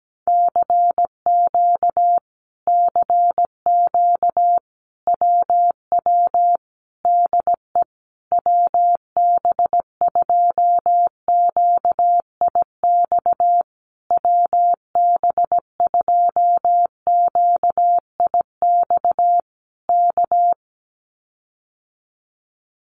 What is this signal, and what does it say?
Morse code, heard as a keyed tone, about 17 wpm, 700 Hz. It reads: CQ CQ WW DE WB2QIX WB2QIX K
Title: Common QSO Phrases Encore 17wpm